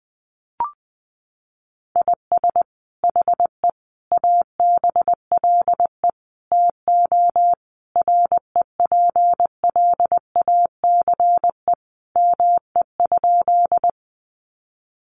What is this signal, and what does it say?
Morse code, heard as a keyed tone, about 20 wpm, 700 Hz. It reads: IS HE ABLE TO REPLACE ME?
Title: Sentences from Top 1400 Words 20wpm